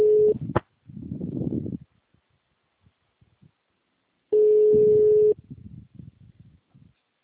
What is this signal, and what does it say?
Ringback tone (Europe, 425 Hz).